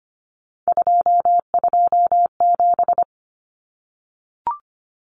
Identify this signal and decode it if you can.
Morse code, heard as a keyed tone, about 25 wpm, 700 Hz. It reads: 227
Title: Common Cut Numbers - 3-Digit 25wpm